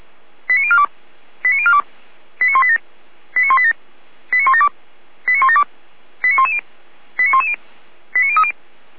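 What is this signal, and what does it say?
Five-tone selective call (ZVEI), heard as quick runs of five tones, tones of about 70 ms. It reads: 89042 89042 89187 89187 89172 89172 89110 89110 89030
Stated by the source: ZVEI five-tone selective call received on SDR.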